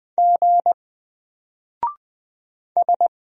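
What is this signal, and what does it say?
Morse code, heard as a keyed tone, about 20 wpm, 700 Hz. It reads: G S
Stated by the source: Single letter and number Morse drill at 20 wpm, from N4LCD.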